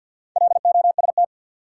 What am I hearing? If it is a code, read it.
Morse code, heard as a keyed tone, about 52 wpm, 700 Hz. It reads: LOST
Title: Sentences from Top 1400 Words - New Word Drill Alt - 50wpm